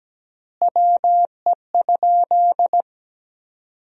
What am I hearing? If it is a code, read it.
Morse code, heard as a keyed tone, about 17 wpm, 700 Hz. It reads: WE?
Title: 3x Word-Spacing - Sentences from Top 200 Words - 17wpm